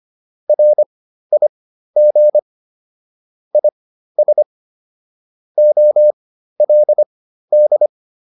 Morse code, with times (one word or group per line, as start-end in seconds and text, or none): 0.5-2.4 RIG
3.5-4.4 IS
5.6-7.9 OLD